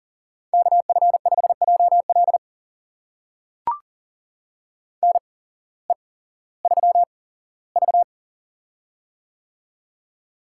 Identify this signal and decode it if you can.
Morse code, heard as a keyed tone, about 40 wpm, spaced as for 19 wpm, 700 Hz. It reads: KF5JL NE3V
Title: USA Call Signs - 8x Character Spacing - 40wpm